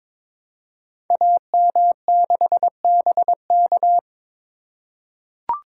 Morse code, heard as a keyed tone, about 22 wpm, 700 Hz. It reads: AM6BK
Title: Code Groups - 5 Letter:Numbers - 22wpm